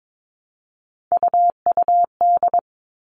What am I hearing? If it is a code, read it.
Morse code, heard as a keyed tone, about 22 wpm, 700 Hz. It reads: UUD
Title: All Cut Numbers - 3-Digit 22wpm